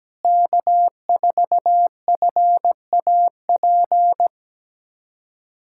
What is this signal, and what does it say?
Morse code, heard as a keyed tone, about 17 wpm, 700 Hz. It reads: K4FAP
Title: USA Callsign DE Callsign 17wpm